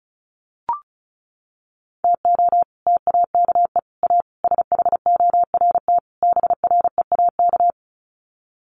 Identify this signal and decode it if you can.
Morse code, heard as a keyed tone, about 34 wpm, 700 Hz. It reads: TO TAKE A SHORT BREAK
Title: Sets of 5 Words Encore2 35wpm